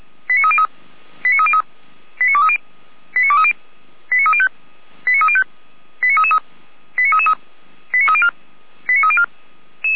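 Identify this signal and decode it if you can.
Five-tone selective call (ZVEI), heard as quick runs of five tones, tones of about 70 ms. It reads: 89303 89303 89230 89230 89305 89305 89333 89333 89304 89304
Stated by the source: ZVEI five-tone call 89315 received on an SDR, channel 455, logged by feuerwehr-orlamuende.org fire brigade.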